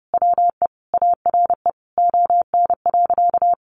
Morse code, heard as a keyed tone, about 30 wpm, 700 Hz. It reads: WE ARE ON.
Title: Sentences from Top 1100 Words 30wpm